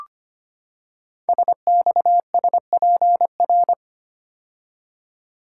Morse code, heard as a keyed tone, about 25 wpm, 700 Hz. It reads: SXSPR
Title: Code Groups - 5 Letters - 25wpm